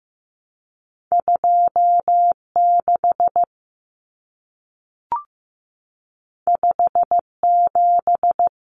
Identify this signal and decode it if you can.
Morse code, heard as a keyed tone, about 15 wpm, 700 Hz. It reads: 26 57
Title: Common Cut Numbers - 2-Digit 15wpm